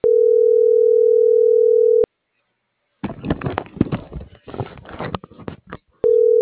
Ringback tone (North America), 440 + 480 Hz.